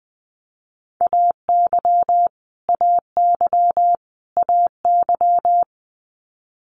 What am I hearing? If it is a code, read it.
Morse code, heard as a keyed tone, about 20 wpm, 700 Hz. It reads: AY AY AY
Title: Multi-letter Phonemes Repeated 3x 20wpm